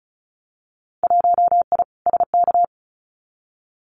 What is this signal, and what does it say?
Morse code, heard as a keyed tone, about 34 wpm, 700 Hz. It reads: 1I SK
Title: Winter Field Day Exchanges Encore 35wpm